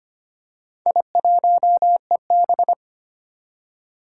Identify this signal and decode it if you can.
Morse code, heard as a keyed tone, about 25 wpm, 700 Hz. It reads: I1EB
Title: Four Challenging Letter-Number Combinations 25wpm